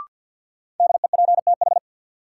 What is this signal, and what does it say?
Morse code, heard as a keyed tone, about 51 wpm, 700 Hz. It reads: DEPTH
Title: Sentences from Top 1200 Words - New Word Drill Alt - 50wpm